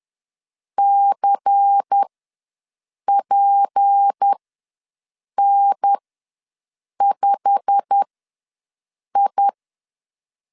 Morse code, heard as a keyed tone, about 10 wpm, spaced as for 6 wpm, 800 Hz. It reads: CPN5I
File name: LEZIONI / 8 YJ45 / 8 YJ45 ws 10wpm 3843